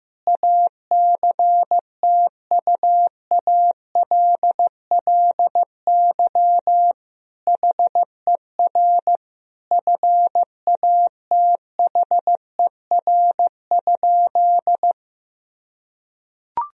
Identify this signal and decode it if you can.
Morse code, heard as a keyed tone, about 15 wpm, 700 Hz. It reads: ACTUALLY HER FATHER?
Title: Sentences from Top 1000 Words - Review Part 4 - 15wpm